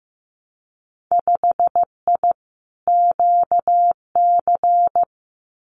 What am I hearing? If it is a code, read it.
Morse code, heard as a keyed tone, about 15 wpm, 700 Hz. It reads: 5I QC
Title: Winter Field Day Exchanges Encore 15wpm